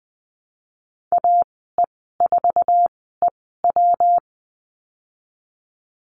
Morse code, heard as a keyed tone, about 20 wpm, spaced as for 15 wpm, 700 Hz. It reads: AE4EW
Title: 154 Lesson 25g - Letter p - Call Signs 20wpm